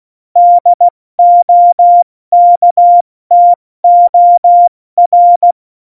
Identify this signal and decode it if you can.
Morse code, heard as a keyed tone, about 16 wpm, 700 Hz. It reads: DOKTOR
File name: Morsekurs / Witze / Witz 09. Januar 2018, 70 F80